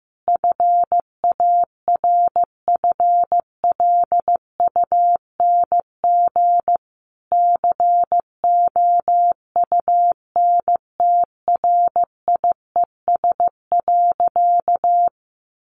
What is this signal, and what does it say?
Morse code, heard as a keyed tone, about 15 wpm, 700 Hz. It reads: FARFLUNG COUNTRIES.
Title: A Sound Of Thunder 15wpm